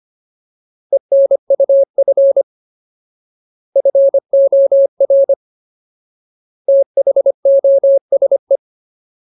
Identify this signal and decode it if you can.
Morse code, heard as a keyed tone, about 25 wpm, 550 Hz. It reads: ENUF FOR THOSE
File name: Ultimate Word List / 25WPM Ultimate